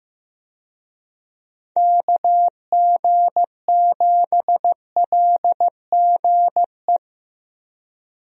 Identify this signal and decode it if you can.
Morse code, heard as a keyed tone, about 15 wpm, 700 Hz. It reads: KG7LGE
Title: USA Call Signs - 6 Characters - Repeated 3x - 15wpm